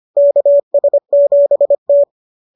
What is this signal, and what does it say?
Morse code, heard as a keyed tone, about 25 wpm, 590 Hz. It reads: KS7T
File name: qso221 25